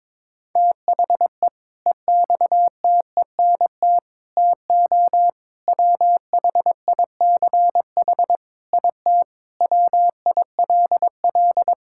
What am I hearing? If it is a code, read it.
Morse code, heard as a keyed tone, about 22 wpm, 700 Hz. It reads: THE EXTENT TO WHICH IT WILL
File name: practice / 22wpm / Sentences with 4-Word Phrases - 22wpm